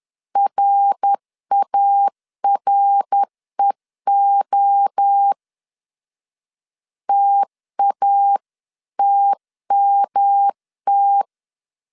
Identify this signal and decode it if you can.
Morse code, heard as a keyed tone, about 10 wpm, 800 Hz. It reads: RAREO TATMT
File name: 10wpm